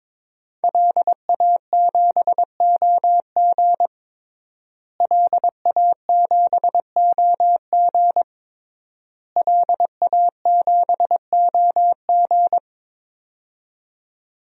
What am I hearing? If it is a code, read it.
Morse code, heard as a keyed tone, about 21 wpm, 700 Hz. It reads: LA7OG LA7OG LA7OG
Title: DXCC-100 Call Signs - 5 Characters - Repeated 3x - 22wpm